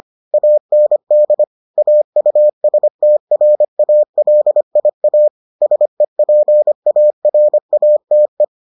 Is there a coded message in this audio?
Yes: Morse code from 0.3 to 8.4 s, heard as a keyed tone, about 25 wpm, 600 Hz. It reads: AND AUSTRALIA SEPARATE